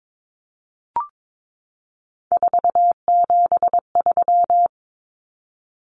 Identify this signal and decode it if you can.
Morse code, heard as a keyed tone, about 22 wpm, 700 Hz. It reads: 473
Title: Numbers - 3-Digit 22wpm